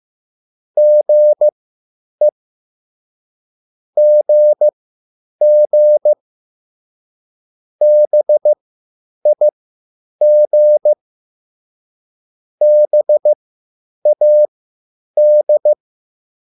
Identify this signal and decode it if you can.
Morse code, heard as a keyed tone, about 15 wpm, spaced as for 9 wpm, 600 Hz. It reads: GE GG BIG BAD
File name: QSL / CW lesson